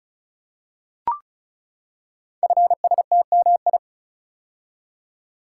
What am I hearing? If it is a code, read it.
Morse code, heard as a keyed tone, about 36 wpm, 690 Hz. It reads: FSTMI
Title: Code Groups - 5 Letters - 35wpm